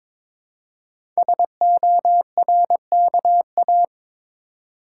Morse code, heard as a keyed tone, about 21 wpm, 700 Hz. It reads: SORKA
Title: Code Groups - 5 Letters - 22wpm